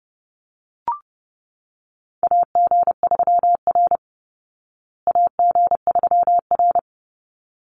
Morse code, heard as a keyed tone, about 30 wpm, 700 Hz. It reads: AG3R AG3R
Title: Call Signs Repeated 3x - 30wpm